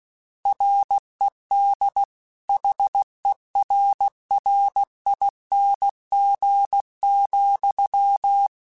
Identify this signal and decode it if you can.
Morse code, heard as a keyed tone, about 16 wpm, 800 Hz. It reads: RED HERRING,
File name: Misc / Misc-mp3s / Word Pairs @15